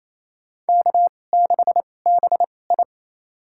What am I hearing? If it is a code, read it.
Morse code, heard as a keyed tone, about 28 wpm, spaced as for 20 wpm, 700 Hz. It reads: K6BI
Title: USA Call Signs - 4 Characters - 2x Character Spacing - 28wpm